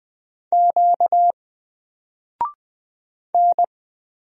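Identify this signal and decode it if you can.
Morse code, heard as a keyed tone, about 20 wpm, 700 Hz. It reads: Q N
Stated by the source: Single letter and number Morse drill at 20 wpm, from N4LCD.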